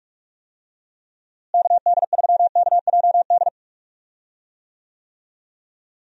Morse code, heard as a keyed tone, about 46 wpm, 700 Hz. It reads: KD3KJD
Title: USA Call Signs - 6 Characters - Repeated 2x - 45wpm